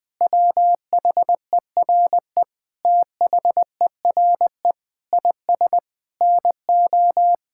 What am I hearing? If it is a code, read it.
Morse code, heard as a keyed tone, about 21 wpm, 700 Hz. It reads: WHERE THERE IS NO
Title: Sentences from Top 1000 Words - Review Part 5 - 20wpm